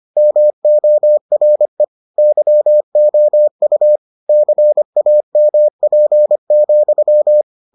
Morse code, heard as a keyed tone, about 25 wpm, 600 Hz. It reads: MORE YOU CAMP,